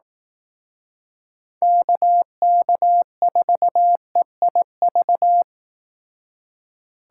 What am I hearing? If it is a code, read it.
Morse code, heard as a keyed tone, about 18 wpm, 700 Hz. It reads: KK4EIV
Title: Speed Racing - USA 2x3 Call Signs 15wpm